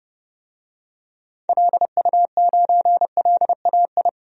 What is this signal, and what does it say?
Morse code, heard as a keyed tone, about 32 wpm, 700 Hz. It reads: LU9LAI
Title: DXCC-100 Call Signs - 6 Characters - 30wpm